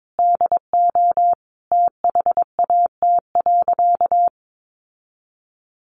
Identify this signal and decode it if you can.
Morse code, heard as a keyed tone, about 22 wpm, 700 Hz. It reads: DO THAT.
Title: Sentences from Top 1500 Words 22wpm